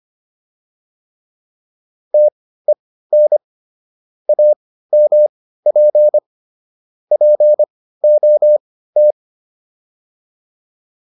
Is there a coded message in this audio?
Yes: Morse code from 2.1 to 9.1 s, heard as a keyed tone, about 25 wpm, spaced as for 15 wpm, 600 Hz. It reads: TEN AMP POT